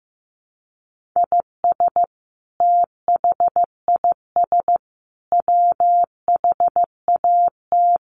Morse code, heard as a keyed tone, about 15 wpm, 700 Hz. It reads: IS THIS WHAT
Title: Sentences from Top 1500 Words 15wpm